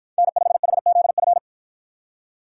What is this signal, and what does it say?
Morse code, heard as a keyed tone, about 47 wpm, 700 Hz. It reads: N5HZF